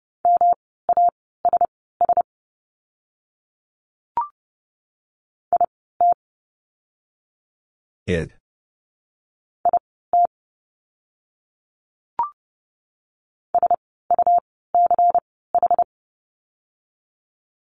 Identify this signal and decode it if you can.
Morse code, heard as a keyed tone, about 30 wpm, spaced as for 17 wpm, 700 Hz. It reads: MASS IT IT SUCH
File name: course / 30:10wpm / 108 Lesson 18f - Letter m - Words 30wpm